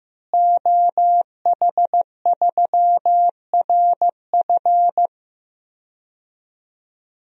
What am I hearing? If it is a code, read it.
Morse code, heard as a keyed tone, about 15 wpm, 700 Hz. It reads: OH3RF
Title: DXCC-100 Call Signs - 5 Characters - Repeated 2x - 15wpm